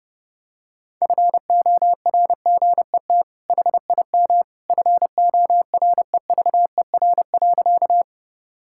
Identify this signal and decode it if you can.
Morse code, heard as a keyed tone, about 30 wpm, 700 Hz. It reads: FORGET HIM FOREVER.
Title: Sentences from Top 1600 Words 30wpm